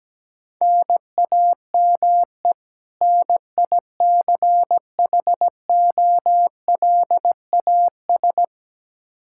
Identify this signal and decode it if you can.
Morse code, heard as a keyed tone, about 17 wpm, 700 Hz. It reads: NAME NICHOLAS
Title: K1USN SST QSO Practice 17wpm